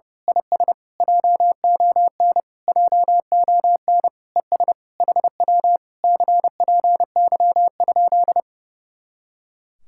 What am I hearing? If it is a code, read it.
Morse code, heard as a keyed tone, about 30 wpm, 700 Hz. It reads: IS JON JON ES HW CPY?
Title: Speed-Racing Common QSO Phrases - 30wpm